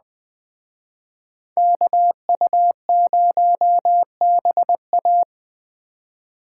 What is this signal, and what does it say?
Morse code, heard as a keyed tone, about 21 wpm, 700 Hz. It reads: KU0BA
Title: Speed Racing - USA 2x2 Call Signs 17wpm